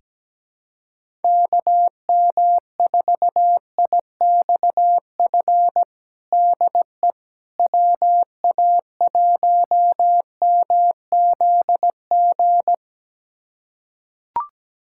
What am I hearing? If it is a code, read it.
Morse code, heard as a keyed tone, about 17 wpm, 700 Hz. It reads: KM4IXF DE WA1MZG
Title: USA Callsign DE Callsign 17wpm